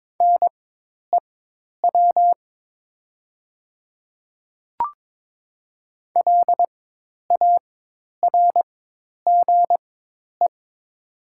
Morse code, heard as a keyed tone, about 22 wpm, spaced as for 10 wpm, 700 Hz. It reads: NEW LARGE
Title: 4x Character Spacing - Top 200 Words - 22wpm